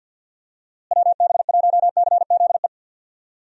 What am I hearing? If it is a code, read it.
Morse code, heard as a keyed tone, about 47 wpm, 700 Hz. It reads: WB1CZE
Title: USA Call Signs - 6 Characters - Repeated 3x - 50wpm